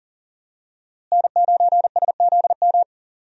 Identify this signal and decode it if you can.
Morse code, heard as a keyed tone, about 40 wpm, 700 Hz. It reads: N9SZM
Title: USA Call Signs - 5 Characters - Repeated 3x - 40wpm